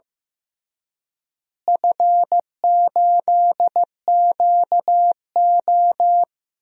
Morse code, heard as a keyed tone, about 15 wpm, 700 Hz. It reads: F8QO
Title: DXCC-100 Call Signs - 4 Characters - Repeated 3x - 15wpm